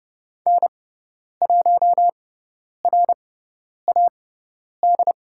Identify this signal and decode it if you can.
Morse code, heard as a keyed tone, about 30 wpm, spaced as for 10 wpm, 700 Hz. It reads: N1RAD
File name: Lesson 17g - Letter u - Call Signs 30wpm